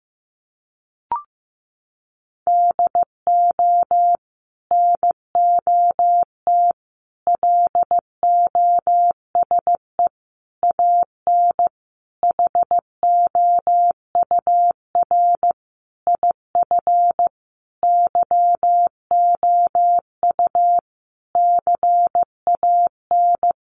Morse code, heard as a keyed tone, about 15 wpm, 700 Hz. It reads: DO NOT LOSE AN HOUR IF YOU CAN
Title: Sentences from Top 1100 Words 15wpm